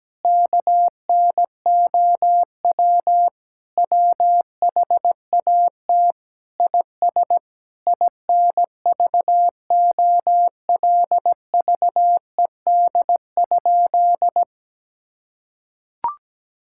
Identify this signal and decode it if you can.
Morse code, heard as a keyed tone, about 17 wpm, 700 Hz. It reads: KNOW WHAT IS INVOLVED?
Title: Sentences from Top 1000 Words - Review Part 5 - 17wpm